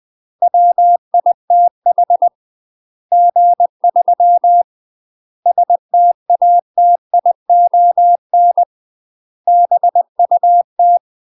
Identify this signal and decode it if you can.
Morse code, heard as a keyed tone, about 20 wpm, 700 Hz. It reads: WITH G3 STATION BUT
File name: UK MASTER QSOs TRAINING / 20WPM / UK MASTER QSOs 0035